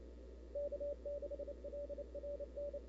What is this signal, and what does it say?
Morse code, heard as a keyed tone, about 27 wpm, 530 Hz. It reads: K6LRN
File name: CW / 2005-k5tr arrl cw dx -00:00